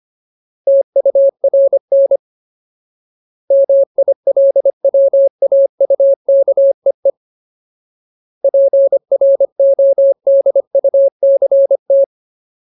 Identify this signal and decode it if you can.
Morse code, heard as a keyed tone, about 25 wpm, 550 Hz. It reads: TURN MILWAUKEE PRODUCT
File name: Ultimate Word List / 25WPM Ultimate